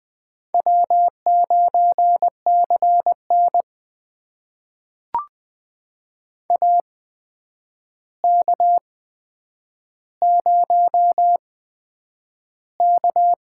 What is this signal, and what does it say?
Morse code, heard as a keyed tone, about 20 wpm, spaced as for 9 wpm, 700 Hz. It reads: W9CN AK0K
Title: USA Call Signs - 4 Characters - 8x Character Spacing - 20wpm